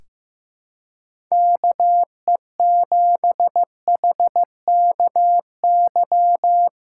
Morse code, heard as a keyed tone, about 15 wpm, 700 Hz. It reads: KE7HKY
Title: USA Callsign DE Callsign - 6 Characters - 15wpm